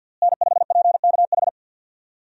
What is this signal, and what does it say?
Morse code, heard as a keyed tone, about 47 wpm, 700 Hz. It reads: N5PKH
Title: USA Call Signs - 5 Characters - Repeated 2x - 50wpm